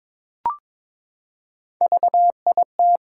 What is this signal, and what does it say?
Morse code, heard as a keyed tone, about 22 wpm, 700 Hz. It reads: VIT
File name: practice / 22wpm / Three Letter-Number Combinations - 22wpm